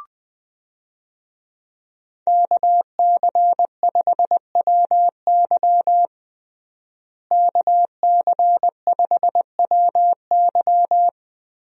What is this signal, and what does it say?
Morse code, heard as a keyed tone, about 20 wpm, 700 Hz. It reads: KC5WY KC5WY
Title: USA Call Signs - 5 Characters - Repeated 3x - 20wpm